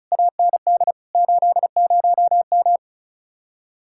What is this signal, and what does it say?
Morse code, heard as a keyed tone, about 40 wpm, 700 Hz. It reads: AND 80M